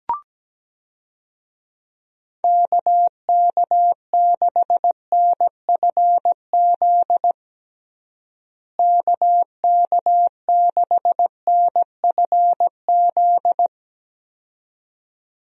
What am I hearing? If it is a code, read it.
Morse code, heard as a keyed tone, about 17 wpm, 700 Hz. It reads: KK6NFZ KK6NFZ
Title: USA Call Signs - 6 Characters - Repeated 2x - 17wpm